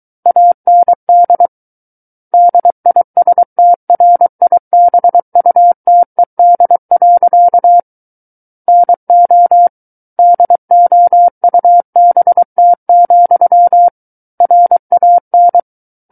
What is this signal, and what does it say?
Morse code, heard as a keyed tone, about 23 wpm, 700 Hz. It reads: AND DISTRIBUTED. NO DOUBT, RAN
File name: Book One / 14-Chapter 14 - 23 WPM